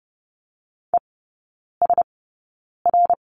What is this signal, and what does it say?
Morse code, heard as a keyed tone, about 30 wpm, spaced as for 9 wpm, 700 Hz. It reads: ESR
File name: Lesson 16e - Letter c - Three Characters 30wpm